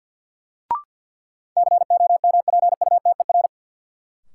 Morse code, heard as a keyed tone, about 53 wpm, 700 Hz. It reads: COMPUTER
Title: Sentences from Top 1000 Words - Review Part 3 - New Word Drill - 50wpm